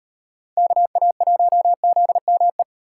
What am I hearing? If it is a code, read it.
Morse code, heard as a keyed tone, about 36 wpm, 700 Hz. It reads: KA1ZME